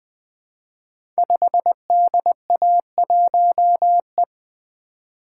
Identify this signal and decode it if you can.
Morse code, heard as a keyed tone, about 20 wpm, 700 Hz. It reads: 5DA1E
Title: Code Groups - 5 Letter:Numbers - 20wpm